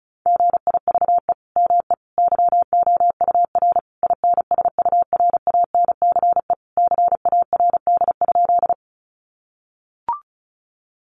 Morse code, heard as a keyed tone, about 34 wpm, 700 Hz. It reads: GIVE ME YOUR INSURANCE CARD?
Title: Sentences from Top 1100 Words 35wpm